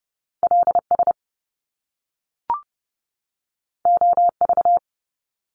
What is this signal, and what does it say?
Morse code, heard as a keyed tone, about 30 wpm, 700 Hz. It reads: LS OV
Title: Top Two Letter Combinations in Words 30wpm